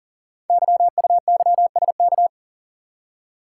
Morse code, heard as a keyed tone, about 39 wpm, 700 Hz. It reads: YUYSK
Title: Code Groups - 5 Letters - 40wpm